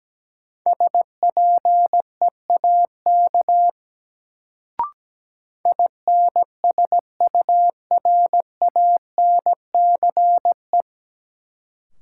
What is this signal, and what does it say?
Morse code, heard as a keyed tone, about 17 wpm, 700 Hz. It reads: SPEAK INSURANCE
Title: Sentences from Top 1100 Words - New Word Drill Alt - 17wpm